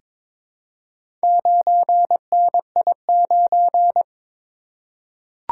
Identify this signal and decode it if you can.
Morse code, heard as a keyed tone, about 22 wpm, 700 Hz. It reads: 9NI9
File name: practice / 22wpm / Four Letter-Number Combinations - 22wpm